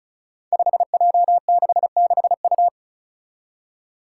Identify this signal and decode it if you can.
Morse code, heard as a keyed tone, about 36 wpm, 700 Hz. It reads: 5J66U
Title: Five Challenging Letter-Number Combinations 35wpm